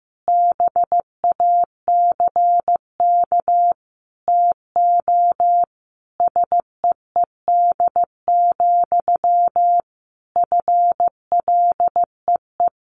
Morse code, heard as a keyed tone, about 15 wpm, 700 Hz. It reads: BACK TO SEED, FLEE
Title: A Sound Of Thunder 15wpm